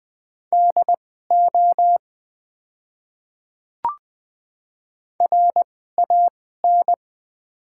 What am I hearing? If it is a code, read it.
Morse code, heard as a keyed tone, about 20 wpm, spaced as for 14 wpm, 690 Hz. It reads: DO RAN
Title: 2x Character Spacing - Top 500 Words - 20wpm